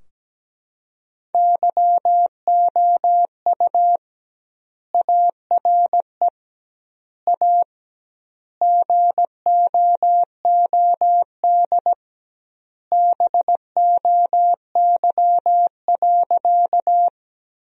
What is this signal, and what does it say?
Morse code, heard as a keyed tone, about 17 wpm, 700 Hz. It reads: YOU ARE A GOOD BOY.
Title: 2x Word-Spacing - Sentences from Top 200 Words - 17wpm